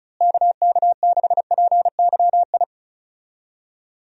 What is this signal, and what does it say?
Morse code, heard as a keyed tone, about 34 wpm, 700 Hz. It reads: KK6PYI
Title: USA Call Signs - 6 Characters - Repeated 3x - 35wpm